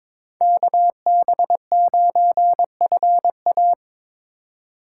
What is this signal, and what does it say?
Morse code, heard as a keyed tone, about 26 wpm, 700 Hz. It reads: KB9FA
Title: Speed Racing - USA 2x2 Call Signs 22wpm